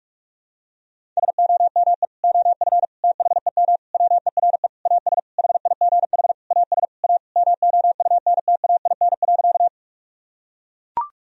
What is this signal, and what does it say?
Morse code, heard as a keyed tone, about 43 wpm, 700 Hz. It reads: SOME OF THEM WERE AS HIGH AS A MOUNTAIN.
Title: Sentences from Top 1000 Words - Review Part 1 - 45wpm